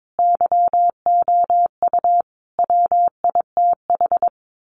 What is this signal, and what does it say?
Morse code, heard as a keyed tone, about 22 wpm, 700 Hz. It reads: YOU WITH
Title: Sets of 5 Words Encore2 22wpm